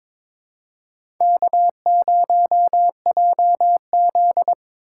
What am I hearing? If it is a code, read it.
Morse code, heard as a keyed tone, about 22 wpm, 700 Hz. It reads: K0JZ
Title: USA Call Signs - 4 Characters - 4x Character Spacing - 22wpm